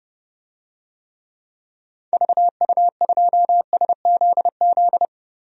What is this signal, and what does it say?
Morse code, heard as a keyed tone, about 30 wpm, 700 Hz. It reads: VU2SZZ
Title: DXCC-100 Call Signs - 6 Characters - 30wpm